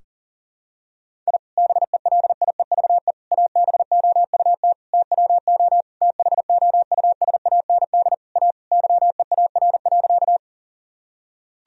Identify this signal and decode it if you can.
Morse code, heard as a keyed tone, about 37 wpm, 700 Hz. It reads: I BELIEVE ABOUT TWO THOUSAND A YEAR.
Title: Sentences from Top 1000 Words - Review Part 3 - 40wpm